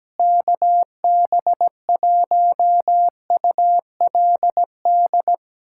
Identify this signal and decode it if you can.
Morse code, heard as a keyed tone, about 17 wpm, 700 Hz. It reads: KB1ULD
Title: USA Callsign DE Callsign - 6 Characters - 17wpm